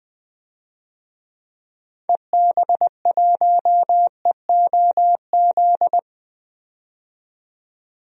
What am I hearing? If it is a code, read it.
Morse code, heard as a keyed tone, about 21 wpm, 700 Hz. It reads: EB1EOZ
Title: DXCC-100 Call Signs - 6 Characters - 20wpm